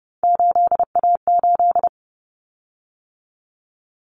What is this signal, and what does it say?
Morse code, heard as a keyed tone, about 30 wpm, 700 Hz. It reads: 8A8
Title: Common Cut Numbers - 3-Digit 30wpm